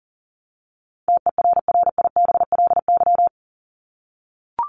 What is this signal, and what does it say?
Morse code, heard as a keyed tone, about 40 wpm, 700 Hz. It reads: TERRIBLY
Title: Top 1751-2000 Words 40wpm